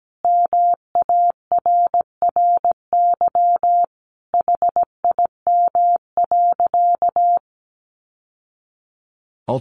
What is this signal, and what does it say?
Morse code, heard as a keyed tone, about 17 wpm, 700 Hz. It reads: MARRY HIM.
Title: Sentences from Top 1200 Words 17wpm